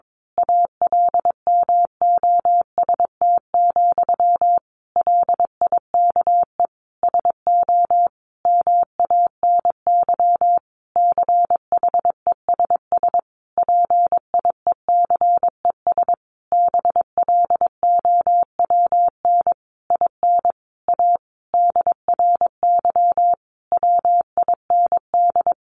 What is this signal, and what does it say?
Morse code, heard as a keyed tone, about 22 wpm, 700 Hz. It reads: ALMOST, LIKE SO MANY CHESS PIECES BLOWN IN A DRY WIND
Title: A Sound Of Thunder 22wpm